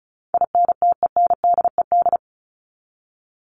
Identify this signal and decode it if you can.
Morse code, heard as a keyed tone, about 34 wpm, 700 Hz. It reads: INTENDED
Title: ED Suffix - Top 100 Words with Base Words 35wpm